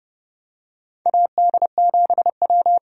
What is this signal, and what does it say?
Morse code, heard as a keyed tone, about 32 wpm, 700 Hz. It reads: AD7W
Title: USA Call Signs - 4 Characters - 30wpm